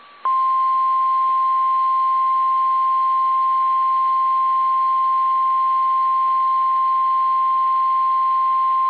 Emergency Alert System (SAME) tones (the 1050 Hz attention tone). It is heard from 0.3 to 8.2 s.